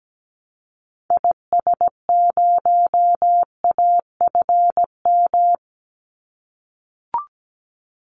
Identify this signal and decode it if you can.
Morse code, heard as a keyed tone, about 17 wpm, 700 Hz. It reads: IS0AFM
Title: Call Signs - 17wpm